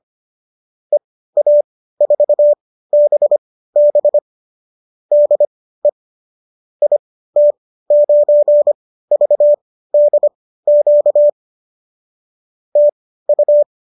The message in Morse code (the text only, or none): EA4BB DE IT9VDQ TU